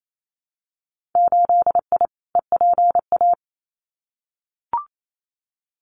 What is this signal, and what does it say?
Morse code, heard as a keyed tone, about 28 wpm, 700 Hz. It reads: 8I EPA
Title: Winter Field Day Exchanges Encore 28wpm